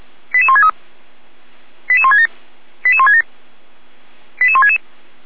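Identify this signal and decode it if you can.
Five-tone selective call (ZVEI), heard as quick runs of five tones, tones of about 70 ms. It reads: 88163 88167 88167 88166
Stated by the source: Five-tone ZVEI selective call received on an SDR.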